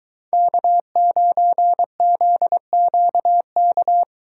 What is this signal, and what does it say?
Morse code, heard as a keyed tone, about 22 wpm, 700 Hz. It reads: K9ZQK